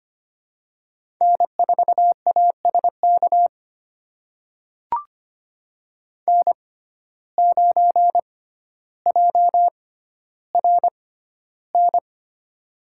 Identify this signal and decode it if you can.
Morse code, heard as a keyed tone, about 25 wpm, spaced as for 13 wpm, 700 Hz. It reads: N4ASK N9JRN
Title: USA Call Signs - 5 Characters - 6x Character Spacing - 25wpm